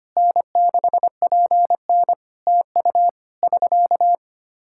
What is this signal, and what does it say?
Morse code, heard as a keyed tone, about 25 wpm, 700 Hz. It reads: N6PN TU <SK>
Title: K1USN SST QSO Practice 25wpm